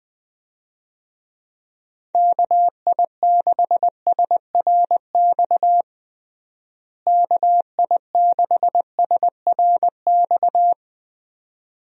Morse code, heard as a keyed tone, about 20 wpm, 700 Hz. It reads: KI6SRX KI6SRX
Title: USA Call Signs - 6 Characters - Repeated 2x - 20wpm